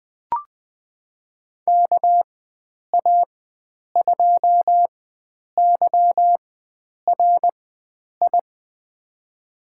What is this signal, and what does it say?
Morse code, heard as a keyed tone, about 20 wpm, spaced as for 9 wpm, 700 Hz. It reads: KA2YRI
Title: USA Call Signs - 6 Characters - 4x Character Spacing - 20wpm